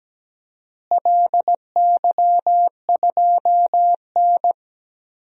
Morse code, heard as a keyed tone, about 17 wpm, 700 Hz. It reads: LY2N
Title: DXCC-100 Call Signs - 4 Characters - Repeated 2x - 17wpm